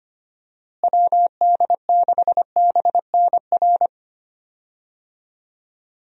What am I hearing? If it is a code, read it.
Morse code, heard as a keyed tone, about 25 wpm, 700 Hz. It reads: WD6BNR
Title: USA Call Signs - 6 Characters - Repeated 2x - 25wpm